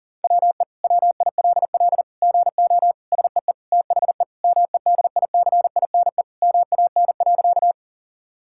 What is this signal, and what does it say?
Morse code, heard as a keyed tone, about 40 wpm, 700 Hz. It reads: WE WILL GO SEE THE MEDICINE MAN.
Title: Sentences from Top 1000 Words - Review Part 6 - 40wpm